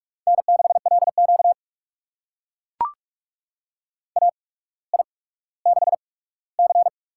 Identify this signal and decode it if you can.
Morse code, heard as a keyed tone, about 44 wpm, spaced as for 19 wpm, 700 Hz. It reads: N6LQ AI6C
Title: USA Call Signs - 4 Characters - 8x Character Spacing - 45wpm